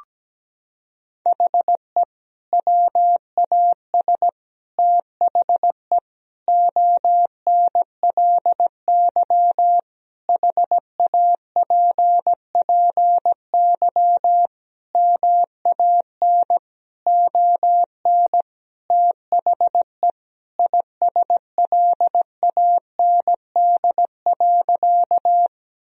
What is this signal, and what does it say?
Morse code, heard as a keyed tone, about 17 wpm, 700 Hz. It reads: HE WAS THE ONLY HAPPY MAN ON THE ISLAND.
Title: Sentences from Top 1000 Words - Review Part 3 - 17wpm